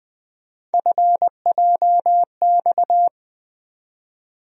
Morse code, heard as a keyed tone, about 20 wpm, 700 Hz. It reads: FJX
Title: USA Call Sign Postfixes - 3 Letters 20wpm